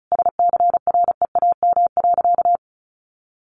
Morse code, heard as a keyed tone, about 34 wpm, 700 Hz. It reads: SCREAM.
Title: A Sound Of Thunder 35wpm